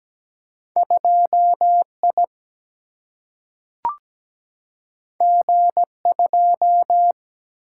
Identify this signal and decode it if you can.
Morse code, heard as a keyed tone, about 17 wpm, 700 Hz. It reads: 2I G2
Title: Single-Letter Single-Number 17wpm